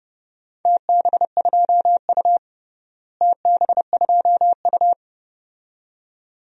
Morse code, heard as a keyed tone, about 32 wpm, 700 Hz. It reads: TB2U TB2U
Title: DXCC-100 Call Signs - 4 Characters - Repeated 2x - 30wpm